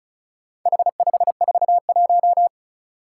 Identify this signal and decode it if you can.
Morse code, heard as a keyed tone, about 34 wpm, 700 Hz. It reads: H541
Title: Four Challenging Letter-Number Combinations 35wpm